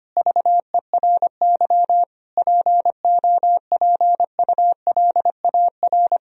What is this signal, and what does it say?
Morse code, heard as a keyed tone, about 25 wpm, 700 Hz. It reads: VERY POPULAR
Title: Sentences from Top 1000 Words - Review Part 3 - 25wpm